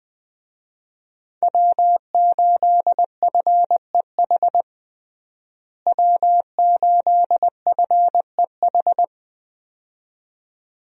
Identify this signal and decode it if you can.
Morse code, heard as a keyed tone, about 20 wpm, 700 Hz. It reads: W8FEH W8FEH
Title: USA Call Signs - 5 Characters - Repeated 2x - 20wpm